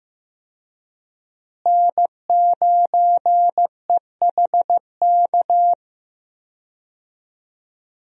Morse code, heard as a keyed tone, about 15 wpm, 700 Hz. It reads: N9EHK
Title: USA Call Signs - 5 Characters - Repeated 3x - 15wpm